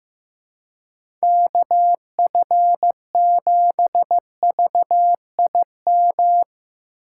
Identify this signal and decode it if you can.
Morse code, heard as a keyed tone, about 15 wpm, 700 Hz. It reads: KF7VIM
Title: USA Call Signs - 6x Character Spacing - 15wpm